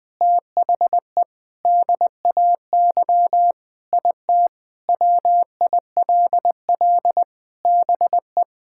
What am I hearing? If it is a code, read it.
Morse code, heard as a keyed tone, about 20 wpm, 700 Hz. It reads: THE DAY IT WILL BE
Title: Sentences with 4-Word Phrases - 20wpm